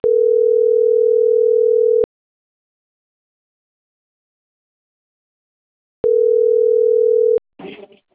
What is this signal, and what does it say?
Ringback tone (North America), 440 + 480 Hz.